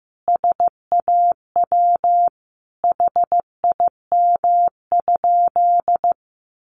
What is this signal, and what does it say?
Morse code, heard as a keyed tone, about 15 wpm, 700 Hz. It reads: SAW HIM?
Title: Sentences from Top 300 Words 15wpm